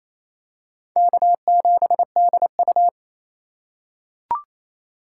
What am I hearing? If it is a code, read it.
Morse code, heard as a keyed tone, about 28 wpm, 700 Hz. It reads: K7DU
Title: USA Call Signs - 4x Character Spacing - 28wpm